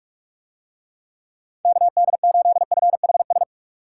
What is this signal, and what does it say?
Morse code, heard as a keyed tone, about 44 wpm, 700 Hz. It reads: KD8FHS